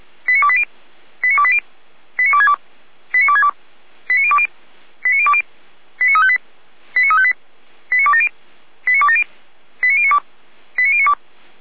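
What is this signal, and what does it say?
Five-tone selective call (ZVEI), heard as quick runs of five tones, tones of about 70 ms. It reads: 89280 89280 89262 89262 89020 89020 89358 89358 89270 89270 89092 89092
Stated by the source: ZVEI five-tone selective call received by SDR.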